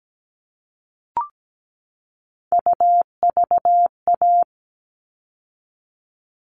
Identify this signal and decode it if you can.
Morse code, heard as a keyed tone, about 17 wpm, 700 Hz. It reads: UVA
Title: All Cut Numbers - 3-Digit 17wpm